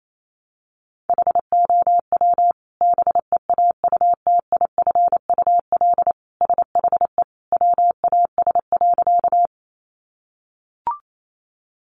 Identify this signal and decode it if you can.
Morse code, heard as a keyed tone, about 28 wpm, 700 Hz. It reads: HOW BEAUTIFUL SHE WAS.
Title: Sentences from Top 900 Words 28wpm